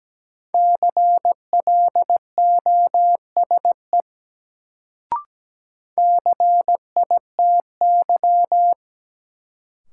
Morse code, heard as a keyed tone, about 17 wpm, 700 Hz. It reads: CLOSE CITY
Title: Sentences from Top 300 Words - New Word Drill - 17wpm